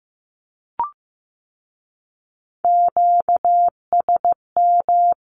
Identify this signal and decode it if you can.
Morse code, heard as a keyed tone, about 15 wpm, 700 Hz. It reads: QSM
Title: Q-Codes with Spoken Meaning - 15wpm